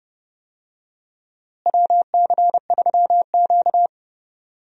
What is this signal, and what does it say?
Morse code, heard as a keyed tone, about 32 wpm, 700 Hz. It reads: WC3Q
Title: USA Call Signs - 4 Characters - Repeated 2x - 30wpm